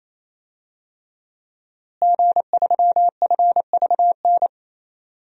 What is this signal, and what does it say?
Morse code, heard as a keyed tone, about 28 wpm, 700 Hz. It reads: G3FVN
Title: DXCC-100 Call Signs - 5 Characters - Repeated 3x - 28wpm